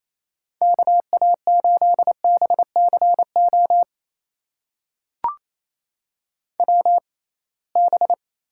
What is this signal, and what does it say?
Morse code, heard as a keyed tone, about 27 wpm, spaced as for 21 wpm, 700 Hz. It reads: KA8BCO WB
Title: USA Call Signs - 6 Characters - 6x Character Spacing - 28wpm